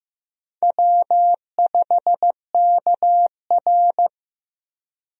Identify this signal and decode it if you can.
Morse code, heard as a keyed tone, about 16 wpm, 700 Hz. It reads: W5KR
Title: USA Call Signs - 4 Characters - Repeated 2x - 15wpm